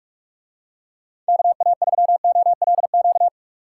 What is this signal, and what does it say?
Morse code, heard as a keyed tone, about 49 wpm, 700 Hz. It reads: KA3OLQ